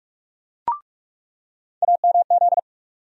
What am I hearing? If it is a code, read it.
Morse code, heard as a keyed tone, about 44 wpm, 700 Hz. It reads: AMZ
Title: USA Call Sign Postfixes - 3 Letters 45wpm